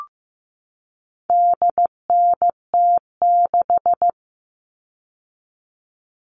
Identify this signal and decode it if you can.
Morse code, heard as a keyed tone, about 15 wpm, 700 Hz. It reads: DNT6
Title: All Cut Numbers - 4-Digit 15wpm